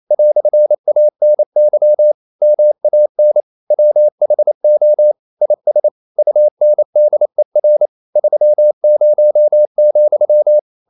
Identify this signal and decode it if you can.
Morse code, heard as a keyed tone, about 28 wpm, 600 Hz. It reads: "ANY MAN WHO IS UNDER 30,
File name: CW / Practice / EnglishQuotations0228